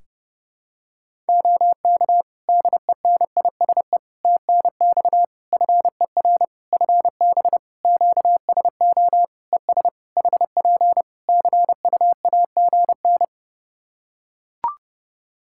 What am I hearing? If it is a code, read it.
Morse code, heard as a keyed tone, about 30 wpm, 700 Hz. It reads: OK DENISE TNX FER FB QSO ES HP CUAGN
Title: K1USN SST QSO Practice 30wpm